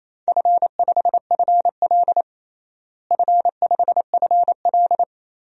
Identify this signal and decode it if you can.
Morse code, heard as a keyed tone, about 28 wpm, 700 Hz. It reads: F5FL F5FL
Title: DXCC-100 Call Signs - 4 Characters - Repeated 3x - 28wpm